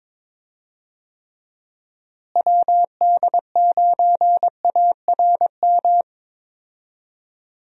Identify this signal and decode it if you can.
Morse code, heard as a keyed tone, about 22 wpm, 700 Hz. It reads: WD9ARM
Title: USA Call Signs - 6 Characters - Repeated 2x - 22wpm